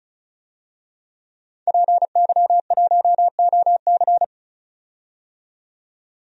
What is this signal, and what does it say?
Morse code, heard as a keyed tone, about 36 wpm, 700 Hz. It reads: PY1OC
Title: DXCC-100 Call Signs - 5 Characters - 35wpm